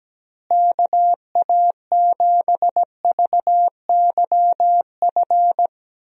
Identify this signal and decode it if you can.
Morse code, heard as a keyed tone, about 17 wpm, 700 Hz. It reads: KA7VYF
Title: USA Call Signs - 6 Characters - Repeated 3x - 17wpm